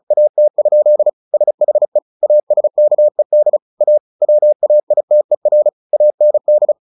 Morse code, heard as a keyed tone, about 34 wpm, 600 Hz. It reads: AT? SHE ASKED A WAITER AND
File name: the old man and the sea 2 35wpm